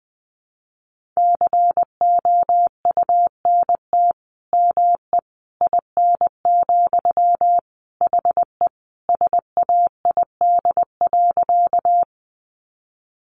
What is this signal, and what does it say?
Morse code, heard as a keyed tone, about 20 wpm, 700 Hz. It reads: COUNT ME IN, HE SAID.
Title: Sentences from Top 1500 Words 20wpm